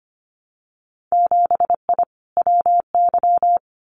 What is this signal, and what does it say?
Morse code, heard as a keyed tone, about 25 wpm, 700 Hz. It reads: 7I WY
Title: Winter Field Day Exchanges Encore 25wpm